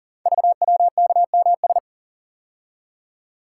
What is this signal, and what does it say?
Morse code, heard as a keyed tone, about 38 wpm, 690 Hz. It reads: VWKMS
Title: Code Groups - 5 Letters - 40wpm